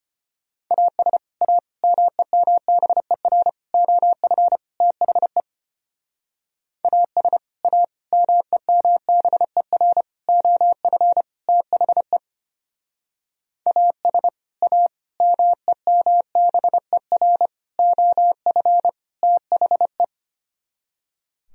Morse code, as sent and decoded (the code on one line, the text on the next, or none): .- ... / .- / -- . -- -... .-. / --- ..-. / - .... / .- ... / .- / -- . -- -... .-. / --- ..-. / - .... / .- ... / .- / -- . -- -... .-. / --- ..-. / - .... .
AS A MEMBER OF THE AS A MEMBER OF THE AS A MEMBER OF THE